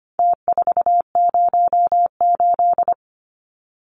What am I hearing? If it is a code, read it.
Morse code, heard as a keyed tone, about 25 wpm, 700 Hz. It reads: T408
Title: Numbers - Contest Style - 4-Digit 25wpm